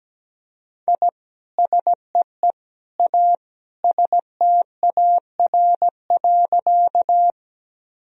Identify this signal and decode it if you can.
Morse code, heard as a keyed tone, about 17 wpm, 700 Hz. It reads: I SEE A STAR.
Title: Sentences from Top 1000 Words - Review Part 2 - 17wpm